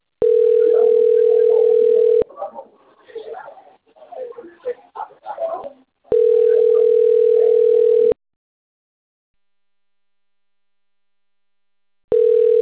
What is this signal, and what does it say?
Ringback tone (North America), 440 + 480 Hz, with speech.
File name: rg-1600-unknown- - - .28125